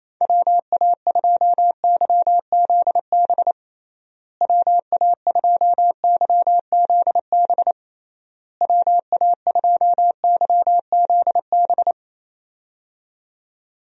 Morse code, heard as a keyed tone, about 27 wpm, 700 Hz. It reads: WA2YZB WA2YZB WA2YZB
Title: USA Call Signs - 6 Characters - Repeated 3x - 28wpm